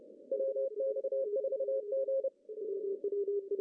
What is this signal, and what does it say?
Morse code, heard as a keyed tone, about 30 wpm, 540 Hz. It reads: WX4G
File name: CW / cqww-cw-k5tr